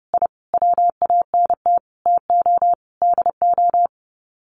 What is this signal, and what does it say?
Morse code, heard as a keyed tone, about 30 wpm, 700 Hz. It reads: I WANT TO DO
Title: Sets of 5 Words Encore 30wpm